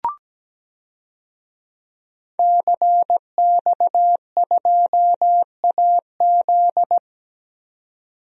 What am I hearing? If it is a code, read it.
Morse code, heard as a keyed tone, about 17 wpm, 700 Hz. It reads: CX2AZ
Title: DXCC-100 Call Signs - 5 Characters - 17wpm